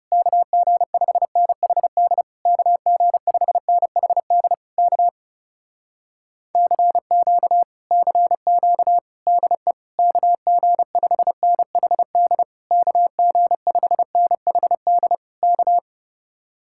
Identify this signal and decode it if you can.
Morse code, heard as a keyed tone, about 32 wpm, 700 Hz. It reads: KG5NHD KG5NHD K CQ CQ DE KG5NHD KG5NHD K
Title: Speed-Racing Common QSO Phrases Encore - 30wpm